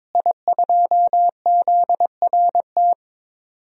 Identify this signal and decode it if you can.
Morse code, heard as a keyed tone, about 21 wpm, 700 Hz. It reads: I2ZRT
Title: DXCC-100 Call Signs - 5 Characters - Repeated 3x - 22wpm